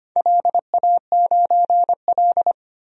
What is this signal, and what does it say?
Morse code, heard as a keyed tone, about 25 wpm, 700 Hz. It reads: LA9L
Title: DXCC-100 Call Signs - 4 Characters - 25wpm